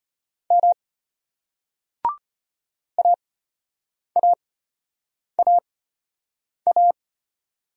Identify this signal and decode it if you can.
Morse code, heard as a keyed tone, about 32 wpm, spaced as for 8 wpm, 700 Hz. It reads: M AAAA